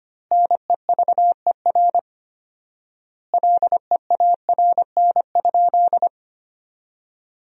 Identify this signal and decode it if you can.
Morse code, heard as a keyed tone, about 25 wpm, 700 Hz. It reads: NEVER LEARN?
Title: 4x Word-Spacing - Sentences from Top 300 Words - 25wpm